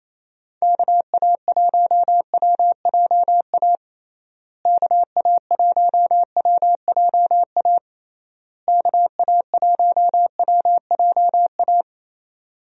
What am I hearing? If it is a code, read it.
Morse code, heard as a keyed tone, about 28 wpm, 700 Hz. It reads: KA1WJA KA1WJA KA1WJA
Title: USA Call Signs - 6 Characters - Repeated 3x - 28wpm